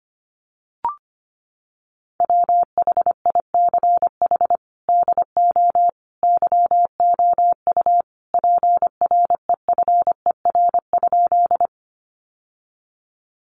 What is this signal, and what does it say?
Morse code, heard as a keyed tone, about 25 wpm, 700 Hz. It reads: WHICH DO YOU PREFER?
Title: Sentences from Top 1100 Words 25wpm